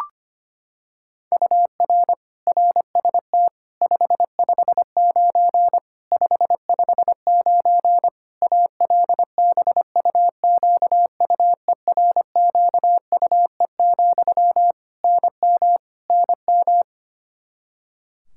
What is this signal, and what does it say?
Morse code, heard as a keyed tone, about 25 wpm, 700 Hz. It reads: UR RST 559 559 ALBUQUERQUE, NM NM
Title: Practice QSO - Exchange 1 - Beginning - 25wpm